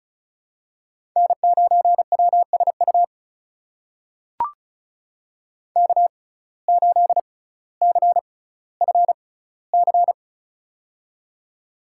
Morse code, heard as a keyed tone, about 36 wpm, spaced as for 18 wpm, 700 Hz. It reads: N9WSU K8CFC
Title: USA Call Signs - 5 Characters - 6x Character Spacing - 35wpm